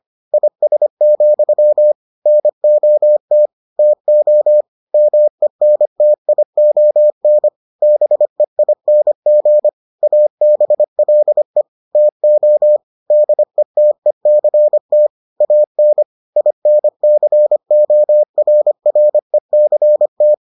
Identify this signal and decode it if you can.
Morse code, heard as a keyed tone, about 25 wpm, 600 Hz. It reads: IS, NOT TO MENTION BEING ABLE TO DETECT AN INCORRECT